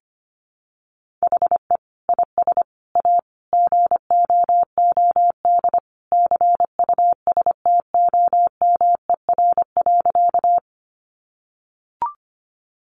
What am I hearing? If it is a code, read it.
Morse code, heard as a keyed tone, about 25 wpm, 700 Hz. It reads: HE IS A GOOD CUSTOMER.
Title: Sentences from Top 1100 Words 25wpm